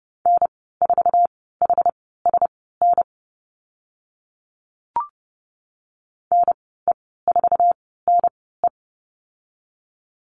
Morse code, heard as a keyed tone, about 30 wpm, spaced as for 17 wpm, 700 Hz. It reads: N4HSN NE4NE
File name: Lesson 14g - Number 2 - Call Signs 30wpm